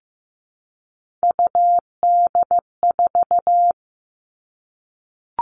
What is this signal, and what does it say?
Morse code, heard as a keyed tone, about 15 wpm, 700 Hz. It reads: UD4
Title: All Cut Numbers - 3-Digit 15wpm